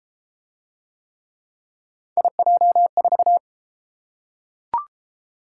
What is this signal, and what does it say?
Morse code, heard as a keyed tone, about 31 wpm, 700 Hz. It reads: IJ4